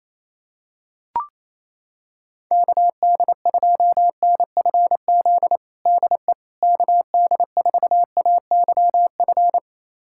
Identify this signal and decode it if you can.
Morse code, heard as a keyed tone, about 28 wpm, 700 Hz. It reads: KD2NFZ DE KD4AYF
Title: USA Callsign DE Callsign - 6 Characters - 28wpm